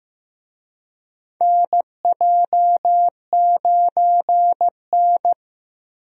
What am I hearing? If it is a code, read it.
Morse code, heard as a keyed tone, about 16 wpm, 700 Hz. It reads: NJ9N
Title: USA Call Signs - 4 Characters - Repeated 2x - 15wpm